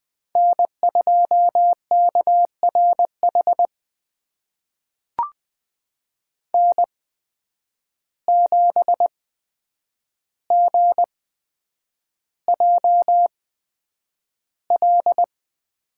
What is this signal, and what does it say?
Morse code, heard as a keyed tone, about 20 wpm, spaced as for 9 wpm, 700 Hz. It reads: N2KRH N7GJL
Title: USA Call Signs - 5 Characters - 8x Character Spacing - 20wpm